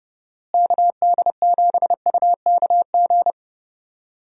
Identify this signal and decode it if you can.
Morse code, heard as a keyed tone, about 30 wpm, 700 Hz. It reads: KD7UKG
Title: USA Call Signs - 6 Characters - Repeated 2x - 30wpm